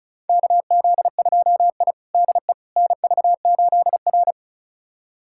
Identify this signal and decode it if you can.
Morse code, heard as a keyed tone, about 36 wpm, 690 Hz. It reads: KZ2I DE NV8R
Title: USA Callsign DE Callsign - 4 Characters - 35wpm